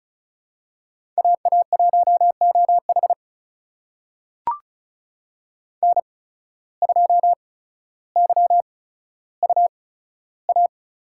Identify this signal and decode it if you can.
Morse code, heard as a keyed tone, about 36 wpm, spaced as for 15 wpm, 700 Hz. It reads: AA1OH N2YUA
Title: USA Call Signs - 5 Characters - 8x Character Spacing - 35wpm